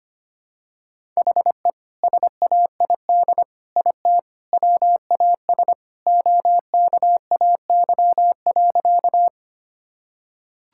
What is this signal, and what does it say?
Morse code, heard as a keyed tone, about 25 wpm, 700 Hz. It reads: HE SAID IT WAS OKAY.
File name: practice / 25wpm / Sentences with 2-Word Phrases - 25wpm